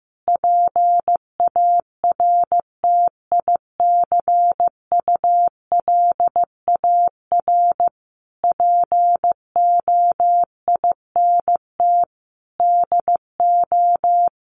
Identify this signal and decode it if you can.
Morse code, heard as a keyed tone, about 15 wpm, 700 Hz. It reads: PARTICULAR POINT DO
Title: Sentences from Top 1300 Words 15wpm